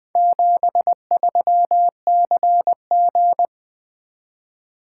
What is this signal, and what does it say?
Morse code, heard as a keyed tone, about 20 wpm, 700 Hz. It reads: 73CG
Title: Four Letter-Number Combinations - 20wpm